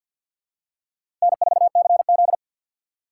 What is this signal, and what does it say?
Morse code, heard as a keyed tone, about 50 wpm, 700 Hz. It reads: N4CZ
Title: USA Call Signs - 6x Character Spacing - 50wpm